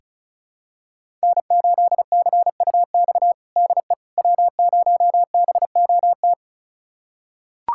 Morse code, heard as a keyed tone, about 36 wpm, 700 Hz. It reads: N8CUX DE W0BOT
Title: USA Callsign DE Callsign - 5 Characters - 35wpm